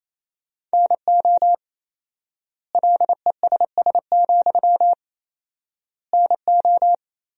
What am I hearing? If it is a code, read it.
Morse code, heard as a keyed tone, about 28 wpm, 700 Hz. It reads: NO LESS, NO
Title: 4x Word-Spacing - Sentences from Top 400 Words - 28wpm